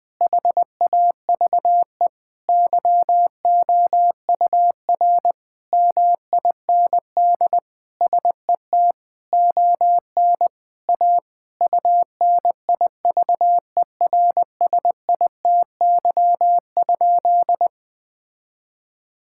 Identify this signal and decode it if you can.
Morse code, heard as a keyed tone, about 21 wpm, 700 Hz. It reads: HAVE YOUR MIND SET ON A UNIVERSITY?
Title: Sentences from Top 1000 Words - Review Part 5 - 20wpm